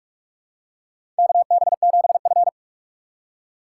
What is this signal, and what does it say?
Morse code, heard as a keyed tone, about 46 wpm, 700 Hz. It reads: KB7F
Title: USA Call Signs - 3x Character Spacing - 45wpm